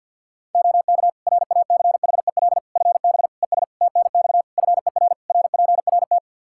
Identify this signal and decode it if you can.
Morse code, heard as a keyed tone, about 46 wpm, 700 Hz. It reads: OK RACHEL FB ES TNX FER RPRT
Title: Practice QSO - Exchange 2 - Info - 50wpm